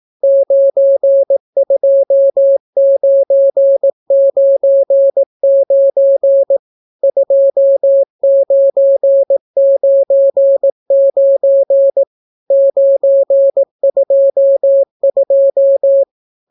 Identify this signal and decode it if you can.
Morse code, heard as a keyed tone, about 18 wpm, 550 Hz. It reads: 92999 2999 922